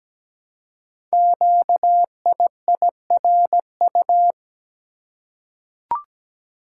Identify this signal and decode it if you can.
Morse code, heard as a keyed tone, about 17 wpm, 700 Hz. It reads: QIIRU
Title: Code Groups - 5 Letters - 17wpm